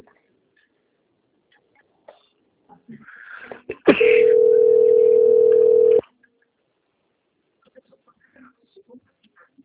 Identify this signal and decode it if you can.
Ringback tone (North America), 440 + 480 Hz.
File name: out- - - - - .36325